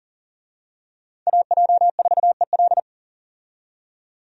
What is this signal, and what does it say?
Morse code, heard as a keyed tone, about 40 wpm, 700 Hz. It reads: AJ4EL
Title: USA Call Signs - 5 Characters - 40wpm